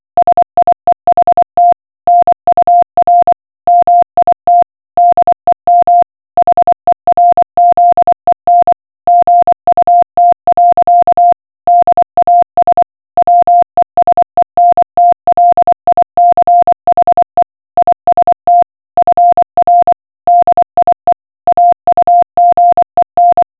Morse code, heard as a keyed tone, about 24 wpm, 700 Hz. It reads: SIEHT NUR MIT DEM HERZEN GUT. DAS WESENTLICHE IST FR DIE AUGEN